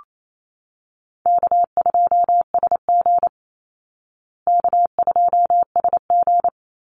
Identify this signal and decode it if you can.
Morse code, heard as a keyed tone, about 28 wpm, 700 Hz. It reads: K2SG K2SG
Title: Easy Call Signs Repeated 3x - 28wpm